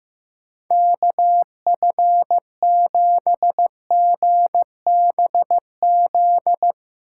Morse code, heard as a keyed tone, about 15 wpm, 700 Hz. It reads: KF7GBZ
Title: USA Call Signs - 6 Characters - 8x Character Spacing - 15wpm